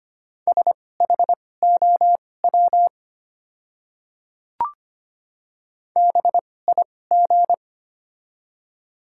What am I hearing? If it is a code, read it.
Morse code, heard as a keyed tone, about 25 wpm, spaced as for 18 wpm, 700 Hz. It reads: SHOW BIG
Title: 2x Character Spacing - Top 300 Words - 25wpm